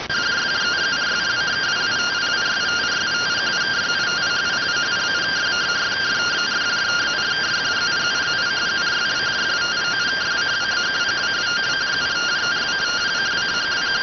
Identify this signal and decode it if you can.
Two-tone teleprinter signal, about 170 Hz shift.